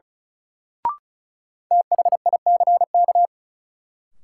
Morse code, heard as a keyed tone, about 36 wpm, 700 Hz. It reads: THICK
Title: Sentences from Top 1600 Words - New Word Drill Alt - 35wpm